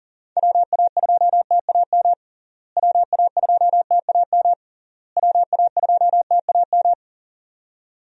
Morse code, heard as a keyed tone, about 40 wpm, 700 Hz. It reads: WA2TAM WA2TAM WA2TAM
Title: USA Call Signs - 6 Characters - Repeated 3x - 40wpm